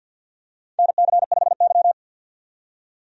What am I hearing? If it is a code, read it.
Morse code, heard as a keyed tone, about 48 wpm, 700 Hz. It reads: NC5Y
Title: USA Call Signs - 4 Characters - 50wpm